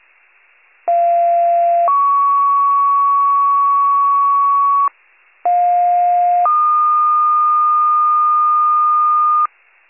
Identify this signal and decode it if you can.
Two-tone page, heard as a one-second tone, then a three-second tone, about 690 then 1120 Hz, about 710 then 1220 Hz. It is heard from 0.9 to 9.5 s.